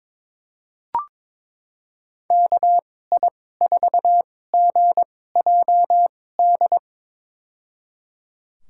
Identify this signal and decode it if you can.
Morse code, heard as a keyed tone, about 22 wpm, spaced as for 16 wpm, 700 Hz. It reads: KI4GJD
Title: USA Call Signs - 6 Characters - 2x Character Spacing - 22wpm